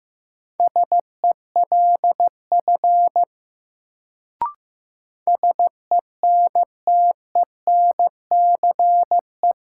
Morse code, heard as a keyed tone, about 15 wpm, 700 Hz. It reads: SELF SENTENCE
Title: Sentences from Top 1000 Words - Review Part 1 - New Word Drill - 15wpm